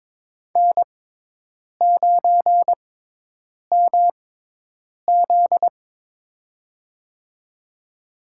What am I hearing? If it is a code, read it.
Morse code, heard as a keyed tone, about 22 wpm, spaced as for 8 wpm, 700 Hz. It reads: N9MZ
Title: USA Call Signs - 4 Characters - 6x Character Spacing - 22wpm